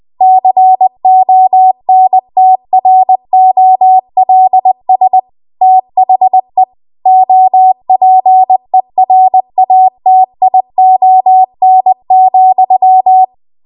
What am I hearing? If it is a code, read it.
Morse code, heard as a keyed tone, about 20 wpm, 750 Hz. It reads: CONTROLS THE OPERATION,
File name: Archive / 20 WPM / 20WPM